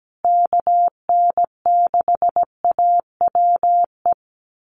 Morse code, heard as a keyed tone, about 17 wpm, 700 Hz. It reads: KN6AWE
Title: USA Call Signs Repeated 3x 17wpm